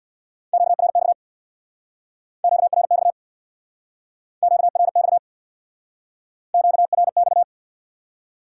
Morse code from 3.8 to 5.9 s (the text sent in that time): QRX